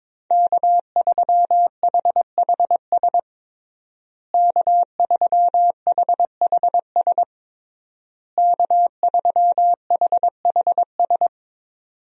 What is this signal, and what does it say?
Morse code, heard as a keyed tone, about 22 wpm, 700 Hz. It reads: K3HHS K3HHS K3HHS
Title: USA Call Signs - 5 Characters - Repeated 3x - 22wpm